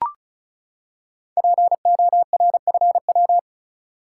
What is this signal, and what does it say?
Morse code, heard as a keyed tone, about 36 wpm, 700 Hz. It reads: PORFW
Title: Code Groups - 5 Letters - 35wpm